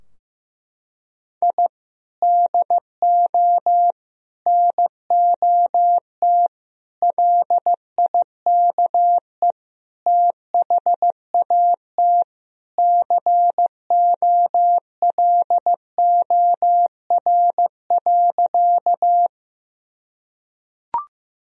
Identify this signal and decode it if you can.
Morse code, heard as a keyed tone, about 15 wpm, 700 Hz. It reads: I DO NOT LIKE THAT COLOR.
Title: Sentences from Top 1000 Words - Review Part 2 - 15wpm